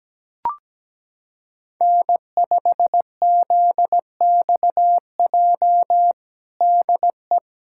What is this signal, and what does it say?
Morse code, heard as a keyed tone, about 17 wpm, 700 Hz. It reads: N5ZXJ DE
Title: USA Callsign DE Callsign - 5 Characters - 17wpm